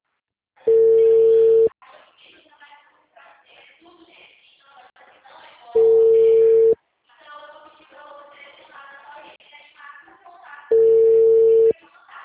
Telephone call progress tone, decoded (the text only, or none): ringback tone